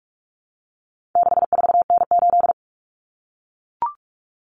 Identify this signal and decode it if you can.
Morse code, heard as a keyed tone, about 44 wpm, 700 Hz. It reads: 64N8
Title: Common Cut Numbers - 4-Digit 45wpm